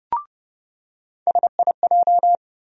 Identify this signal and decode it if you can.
Morse code, heard as a keyed tone, about 30 wpm, 700 Hz. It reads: SIJ
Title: USA Call Sign Postfixes - 3 Letters 30wpm